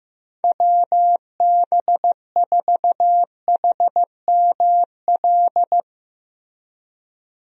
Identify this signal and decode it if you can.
Morse code, heard as a keyed tone, about 16 wpm, 700 Hz. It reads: WB4HML
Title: USA Call Signs - 6 Characters - Repeated 2x - 15wpm